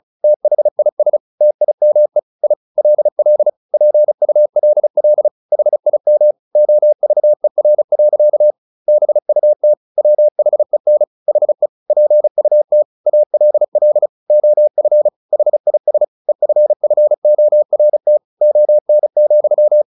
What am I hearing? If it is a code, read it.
Morse code, heard as a keyed tone, about 34 wpm, 600 Hz. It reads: THIS TIME I LL PULL HIM OVER. BUT WHEN HE PUT ALL OF HIS EFFORT ON,
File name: the old man and the sea 2 35wpm